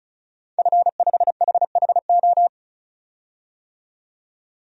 Morse code, heard as a keyed tone, about 36 wpm, 700 Hz. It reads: F5HHO
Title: DXCC-100 Call Signs - 5 Characters - 35wpm